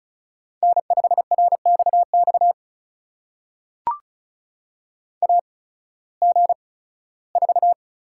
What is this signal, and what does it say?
Morse code, heard as a keyed tone, about 36 wpm, spaced as for 19 wpm, 700 Hz. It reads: N5RXX AG4
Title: USA Call Signs - 5 Characters - 8x Character Spacing - 35wpm